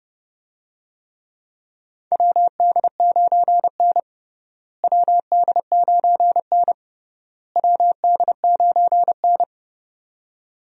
Morse code, heard as a keyed tone, about 30 wpm, 700 Hz. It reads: WD9N WD9N WD9N
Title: USA Call Signs - 4 Characters - Repeated 3x - 30wpm